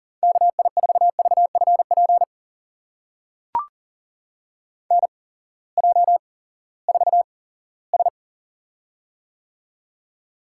Morse code, heard as a keyed tone, about 40 wpm, spaced as for 20 wpm, 700 Hz. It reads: KI4VFP NJ4S
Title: USA Call Signs - 8x Character Spacing - 40wpm